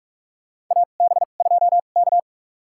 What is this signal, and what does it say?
Morse code, heard as a keyed tone, about 46 wpm, spaced as for 33 wpm, 700 Hz. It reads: AB2K
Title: USA Call Signs - 4 Characters - 2x Character Spacing - 45wpm